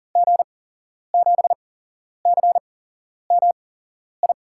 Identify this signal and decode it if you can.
Morse code, heard as a keyed tone, about 40 wpm, 700 Hz. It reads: G 7 C M I
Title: USA Call Signs - 6 Characters - 8x Character Spacing - 40wpm